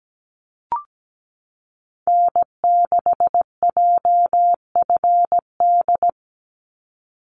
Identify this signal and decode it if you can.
Morse code, heard as a keyed tone, about 17 wpm, 700 Hz. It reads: N6JFD
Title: NA SOTA Activator Callsigns 17wpm